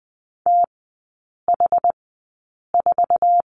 Morse code, heard as a keyed tone, about 20 wpm, spaced as for 8 wpm, 700 Hz. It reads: TH4
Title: 107 Lesson 18e - Letter m - Three Characters 20wpm